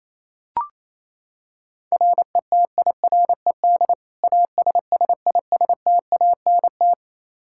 Morse code, heard as a keyed tone, about 28 wpm, 700 Hz. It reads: RETIRED ASSISTANT
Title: K1USN SST QSO Practice 28wpm